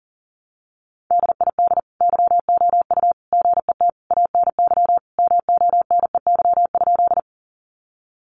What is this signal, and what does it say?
Morse code, heard as a keyed tone, about 40 wpm, 700 Hz. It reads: DID YOU GET ANY MONEY?
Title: Sentences from Top 400 Words 40wpm